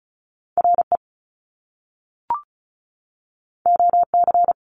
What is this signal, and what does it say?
Morse code, heard as a keyed tone, about 36 wpm, 700 Hz. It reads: RE OC
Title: Top Two Letter Combinations in Words 35wpm